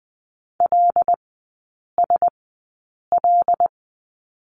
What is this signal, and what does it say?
Morse code, heard as a keyed tone, about 19 wpm, spaced as for 8 wpm, 700 Hz. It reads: LSL
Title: 100 Lesson 17e - Letter u - Three Characters 20wpm